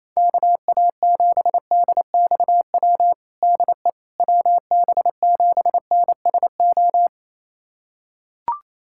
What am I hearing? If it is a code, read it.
Morse code, heard as a keyed tone, about 28 wpm, 700 Hz. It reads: KA7DXW DE WB7NSO
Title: USA Callsign DE Callsign 28wpm